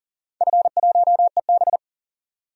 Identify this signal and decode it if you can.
Morse code, heard as a keyed tone, about 40 wpm, 700 Hz. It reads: F1EB
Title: DXCC-100 Call Signs - 4 Characters - 40wpm